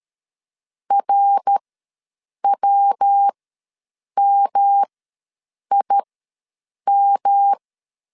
Morse code, heard as a keyed tone, about 13 wpm, spaced as for 7 wpm, 800 Hz. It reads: RWMIM
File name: LEZIONI / 6 PKX0 / 6 PK ws 12wpm 3116